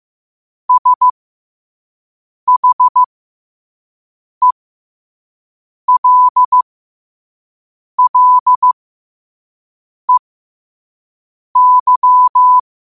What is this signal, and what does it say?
Morse code, heard as a keyed tone, about 15 wpm, spaced as for 5 wpm, 1000 Hz. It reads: SHELLEY
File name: qotdfiles / 5wpm